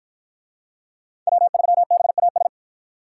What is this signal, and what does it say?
Morse code, heard as a keyed tone, about 53 wpm, 700 Hz. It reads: W3BAS